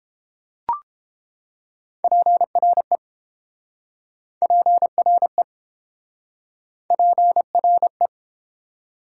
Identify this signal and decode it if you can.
Morse code, heard as a keyed tone, about 32 wpm, 700 Hz. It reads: PRE PRE PRE